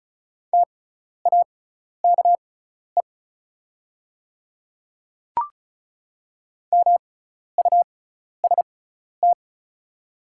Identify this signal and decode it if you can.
Morse code, heard as a keyed tone, about 36 wpm, spaced as for 12 wpm, 700 Hz. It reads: TAKE MUST
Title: 6x Character Spacing - Top 200 Words - 35wpm